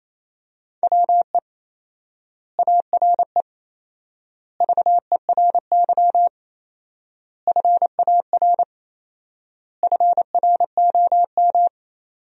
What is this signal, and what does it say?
Morse code, heard as a keyed tone, about 28 wpm, 700 Hz. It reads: WE ARE VERY FAR FROM
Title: 4x Word-Spacing - Sentences from Top 300 Words - 28wpm